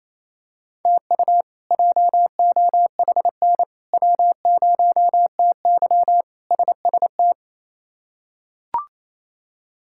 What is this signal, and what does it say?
Morse code, heard as a keyed tone, about 28 wpm, 700 Hz. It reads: TU JOHN W0TY SST
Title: K1USN SST QSO Practice - 28wpm